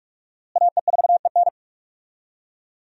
Morse code, heard as a keyed tone, about 42 wpm, 700 Hz. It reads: AE4EN
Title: USA Call Signs - 5 Characters - Repeated 3x - 45wpm